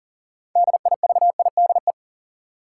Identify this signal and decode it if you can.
Morse code, heard as a keyed tone, about 40 wpm, 700 Hz. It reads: DIVIDE
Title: Sentences from Top 1600 Words - New Word Drill - 40wpm